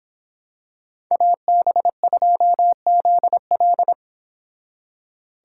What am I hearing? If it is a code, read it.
Morse code, heard as a keyed tone, about 26 wpm, 700 Hz. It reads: AB2ZL